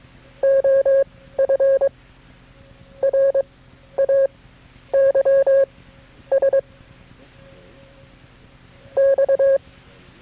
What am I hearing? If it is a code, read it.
Morse code, heard as a keyed tone, about 24 wpm, spaced as for 11 wpm, 540 Hz. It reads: OF RAYS X
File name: KB6MT-tape3-side1